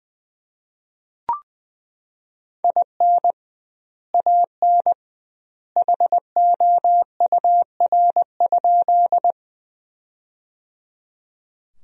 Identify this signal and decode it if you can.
Morse code, heard as a keyed tone, about 20 wpm, 700 Hz. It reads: IN AN HOUR?
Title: 2x Word-Spacing - Sentences from Top 400 Words - 20wpm